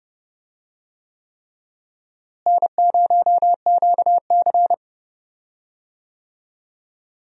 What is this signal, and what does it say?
Morse code, heard as a keyed tone, about 32 wpm, 700 Hz. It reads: N0QC
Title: USA Call Signs - 4 Characters - Repeated 3x - 30wpm